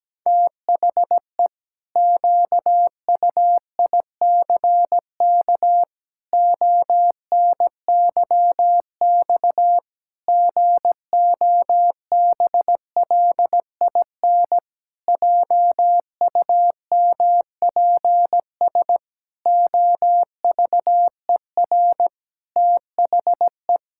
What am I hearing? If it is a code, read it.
Morse code, heard as a keyed tone, about 17 wpm, 700 Hz. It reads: THE QUICK ONYX GOBLIN JUMPS OVER THE
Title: Pangram Challenge v2 - 17wpm